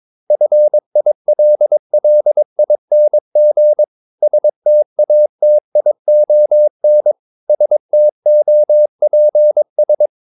Morse code, heard as a keyed tone, about 22 wpm, 600 Hz. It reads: FILLING STATION STOPS
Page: Romanen Never af Ken Follet på engelsk.